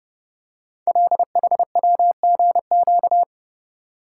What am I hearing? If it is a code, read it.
Morse code, heard as a keyed tone, about 30 wpm, 690 Hz. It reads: LHWGQ
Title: Code Groups - 5 Letters - 30wpm